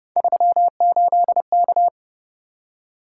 Morse code, heard as a keyed tone, about 30 wpm, 700 Hz. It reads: 38K
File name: Speed-Racing - Three Letter-Number Combinations - 25wpm